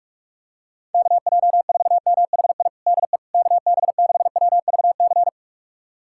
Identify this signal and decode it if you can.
Morse code, heard as a keyed tone, about 41 wpm, 700 Hz. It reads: KJ4MHI DE KB6WVC
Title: USA Callsign DE Callsign - 6 Characters - 45wpm